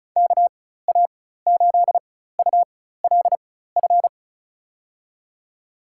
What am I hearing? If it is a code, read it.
Morse code, heard as a keyed tone, about 36 wpm, spaced as for 16 wpm, 700 Hz. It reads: KA8ULF
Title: USA Call Signs - 4x Character Spacing - 35wpm